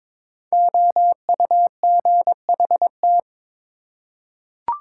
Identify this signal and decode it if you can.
Morse code, heard as a keyed tone, about 22 wpm, 700 Hz. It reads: OUGHT
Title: Five Letter Words - 22wpm